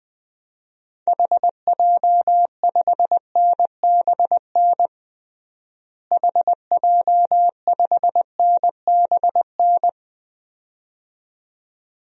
Morse code, heard as a keyed tone, about 21 wpm, 700 Hz. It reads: HJ5NBN HJ5NBN
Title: DXCC-100 Call Signs - 6 Characters - Repeated 2x - 20wpm